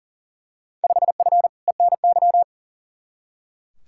Morse code, heard as a keyed tone, about 40 wpm, 700 Hz. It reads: 5F ENY
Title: Field Day Exchanges Encore - 40wpm